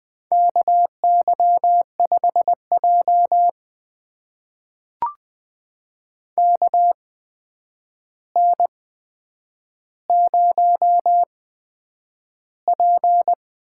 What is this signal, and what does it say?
Morse code, heard as a keyed tone, about 20 wpm, spaced as for 9 wpm, 700 Hz. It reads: KY5J KN0P
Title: USA Call Signs - 4 Characters - 8x Character Spacing - 20wpm